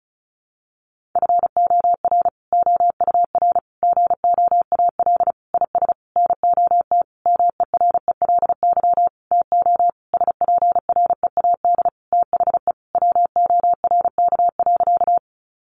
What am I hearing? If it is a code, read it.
Morse code, heard as a keyed tone, about 34 wpm, 700 Hz. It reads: FOR OUR GOAL IS NOT MERELY TO SPREAD THE WORK.
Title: Sentences from Top 1400 Words 35wpm